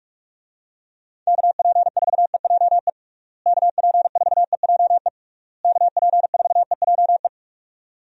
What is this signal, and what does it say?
Morse code, heard as a keyed tone, about 46 wpm, 700 Hz. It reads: KP4EJE KP4EJE KP4EJE
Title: DXCC-100 Call Signs - 6 Characters - Repeated 3x - 45wpm